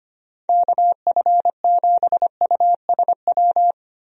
Morse code, heard as a keyed tone, about 25 wpm, 700 Hz. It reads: KF7USW
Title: USA Call Signs - 6 Characters - 25wpm